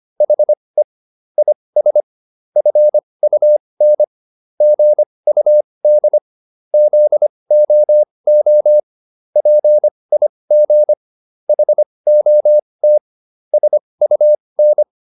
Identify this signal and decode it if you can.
Morse code, heard as a keyed tone, about 25 wpm, spaced as for 20 wpm, 600 Hz. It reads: HE IS FUN GUD ZOO PIG HOT SUN